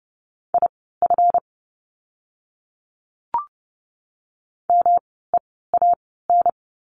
Morse code, heard as a keyed tone, about 30 wpm, spaced as for 17 wpm, 700 Hz. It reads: IF MEAN
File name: course / 30:10wpm / 230 Lesson 37f - Letter x - Words 30wpm